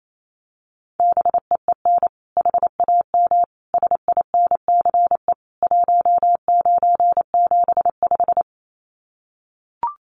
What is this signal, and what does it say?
Morse code, heard as a keyed tone, about 28 wpm, 700 Hz. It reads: BEEN HAM SINCE 1975
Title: Common QSO Phrases 28wpm